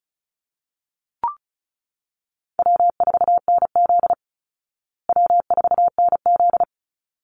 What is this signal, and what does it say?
Morse code, heard as a keyed tone, about 34 wpm, 700 Hz. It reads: W4NZ W4NZ
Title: Call Signs Repeated 3x - 35wpm